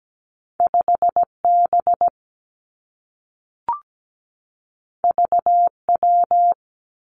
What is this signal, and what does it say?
Morse code, heard as a keyed tone, about 17 wpm, 700 Hz. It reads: 5B VW
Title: Two Challenging Letter-Number Combinations 17wpm